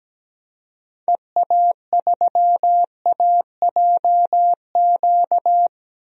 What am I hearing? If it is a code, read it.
Morse code, heard as a keyed tone, about 17 wpm, 700 Hz. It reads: EA3AJQ
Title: DXCC-100 Call Signs - 6 Characters - 17wpm